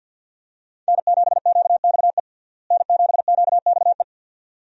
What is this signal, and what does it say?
Morse code, heard as a keyed tone, about 51 wpm, 700 Hz. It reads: N7QXE N7QXE
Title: USA Call Signs - 5 Characters - Repeated 2x - 50wpm